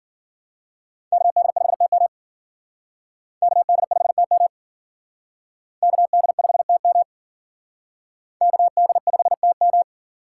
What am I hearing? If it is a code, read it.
Morse code, heard as a keyed tone, about 52 wpm, 700 Hz. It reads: KD5TM KD5TM KD5TM KD5TM